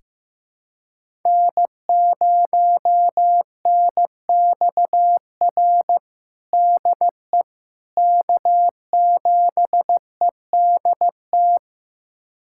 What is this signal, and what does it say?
Morse code, heard as a keyed tone, about 15 wpm, 700 Hz. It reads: N0NXR DE K7EDT
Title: USA Callsign DE Callsign - 5 Characters - 15wpm